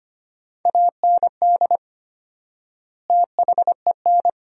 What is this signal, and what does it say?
Morse code, heard as a keyed tone, about 25 wpm, 700 Hz. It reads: AND THEN
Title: 4x Word-Spacing - Sentences from Top 300 Words - 25wpm